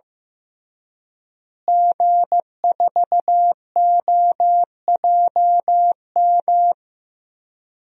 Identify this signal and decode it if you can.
Morse code, heard as a keyed tone, about 15 wpm, 700 Hz. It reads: G4OJM
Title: DXCC-100 Call Signs - 5 Characters - Repeated 3x - 15wpm